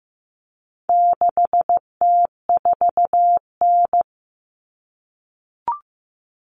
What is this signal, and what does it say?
Morse code, heard as a keyed tone, about 15 wpm, 700 Hz. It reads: 6T4N
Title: Common Cut Numbers - 4-Digit 15wpm